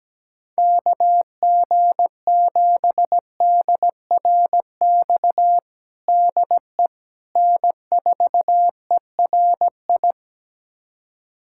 Morse code, heard as a keyed tone, about 17 wpm, 700 Hz. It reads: KG7DRX DE N4ERI
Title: USA Callsign DE Callsign 17wpm